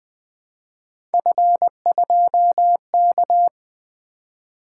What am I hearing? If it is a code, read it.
Morse code, heard as a keyed tone, about 20 wpm, 700 Hz. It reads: F2K
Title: Three Letter-Number Combinations - 20wpm